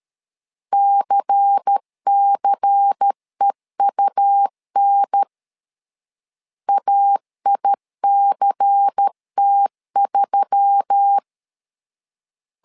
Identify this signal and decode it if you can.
Morse code, heard as a keyed tone, about 13 wpm, 800 Hz. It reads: CCEUN AICT3
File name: LEZIONI / 5 LBVC / 5 LBVC 12wpm 1955